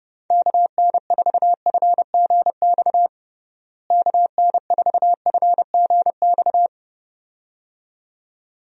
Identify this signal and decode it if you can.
Morse code, heard as a keyed tone, about 30 wpm, 700 Hz. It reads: KN4FGX KN4FGX
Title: USA Call Signs - 6 Characters - Repeated 2x - 30wpm